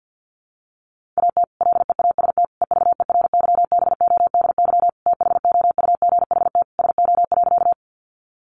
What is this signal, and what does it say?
Morse code, heard as a keyed tone, about 50 wpm, 700 Hz. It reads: AT LEAST EVERYBODY THOUGHT SO.
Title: Sentences from Top 1400 Words 50wpm